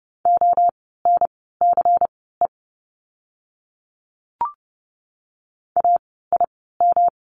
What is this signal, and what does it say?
Morse code, heard as a keyed tone, about 30 wpm, spaced as for 17 wpm, 700 Hz. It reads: ONCE AIM
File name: course / 30:10wpm / 108 Lesson 18f - Letter m - Words 30wpm